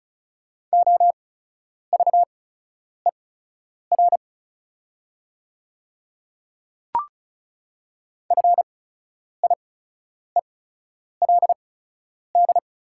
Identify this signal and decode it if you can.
Morse code, heard as a keyed tone, about 36 wpm, spaced as for 10 wpm, 690 Hz. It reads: OVER FIELD
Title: 8x Character Spacing - Top 500 Words - 35wpm